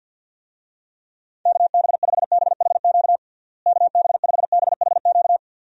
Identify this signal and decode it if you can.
Morse code, heard as a keyed tone, about 50 wpm, 700 Hz. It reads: KB5BHQ KB5BHQ
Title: USA Call Signs - 6 Characters - Repeated 3x - 50wpm